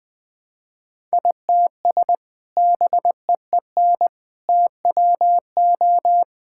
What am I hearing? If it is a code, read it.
Morse code, heard as a keyed tone, about 20 wpm, 700 Hz. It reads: ITS BEEN TWO
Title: Sentences with 5-Word Phrases - 20wpm